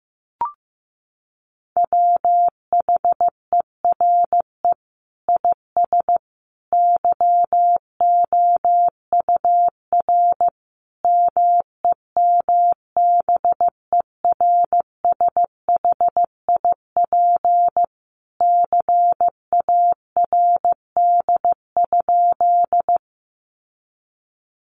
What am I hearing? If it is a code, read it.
Morse code, heard as a keyed tone, about 15 wpm, 700 Hz. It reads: WHERE IS YOUR MEMBERSHIP CARD?
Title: Sentences from Top 1400 Words 15wpm